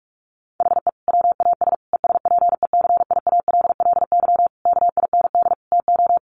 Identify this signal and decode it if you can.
Morse code, heard as a keyed tone, about 44 wpm, 700 Hz. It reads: HE WAS ESPECIALLY KIND TO
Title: Sentences from Top 700 Words 45wpm